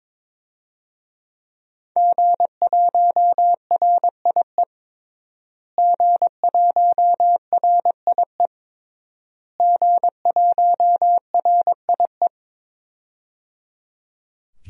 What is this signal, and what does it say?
Morse code, heard as a keyed tone, about 21 wpm, 700 Hz. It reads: G1RIE G1RIE G1RIE
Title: DXCC-100 Call Signs - 5 Characters - Repeated 3x - 22wpm